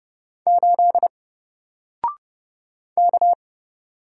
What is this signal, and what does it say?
Morse code, heard as a keyed tone, about 30 wpm, 700 Hz. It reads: 8 K
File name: Single-Letter-Number-30wpm